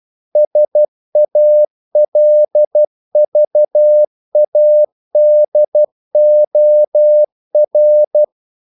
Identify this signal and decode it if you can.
Morse code, heard as a keyed tone, about 12 wpm, 600 Hz. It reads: SALVADOR
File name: CW / Practice / EnglishQuotations0112